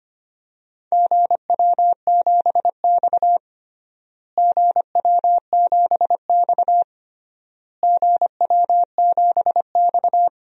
Morse code, heard as a keyed tone, about 25 wpm, 700 Hz. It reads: GW7X GW7X GW7X
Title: DXCC-100 Call Signs - 4 Characters - Repeated 3x - 25wpm